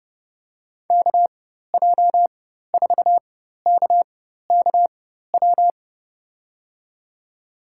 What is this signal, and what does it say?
Morse code, heard as a keyed tone, about 30 wpm, spaced as for 14 wpm, 700 Hz. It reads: KJ4KKW
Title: USA Call Signs - 6 Characters - 4x Character Spacing - 30wpm